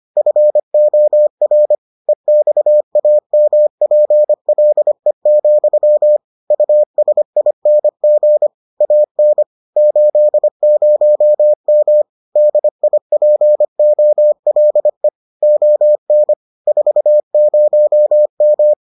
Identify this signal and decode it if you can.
Morse code, heard as a keyed tone, about 25 wpm, 600 Hz. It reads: FOR EXAMPLE, USING AN 80M DIPOLE ON 40M